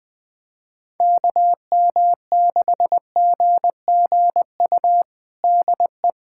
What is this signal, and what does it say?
Morse code, heard as a keyed tone, about 20 wpm, 700 Hz. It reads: KM6GGU DE
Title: USA Callsign DE Callsign 20wpm